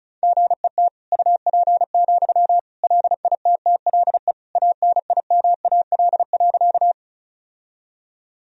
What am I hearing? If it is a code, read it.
Morse code, heard as a keyed tone, about 36 wpm, 700 Hz. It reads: GET UP, LITTLE ANIMAL.
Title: Sentences from Top 1000 Words - Review Part 1 - 35wpm